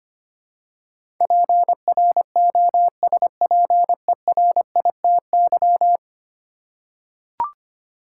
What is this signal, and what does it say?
Morse code, heard as a keyed tone, about 25 wpm, 700 Hz. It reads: PROSPERITY
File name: Ten Letter Words - 25wpm